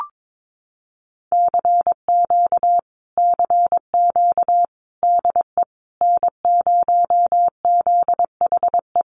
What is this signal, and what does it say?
Morse code, heard as a keyed tone, about 22 wpm, 700 Hz. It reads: CQ CQ DE N0ZHE
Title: Common QSO Phrases Encore 22wpm